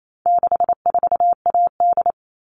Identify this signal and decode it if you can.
Morse code, heard as a keyed tone, about 28 wpm, 700 Hz. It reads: 64AD
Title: All Cut Numbers - 4-Digit 28wpm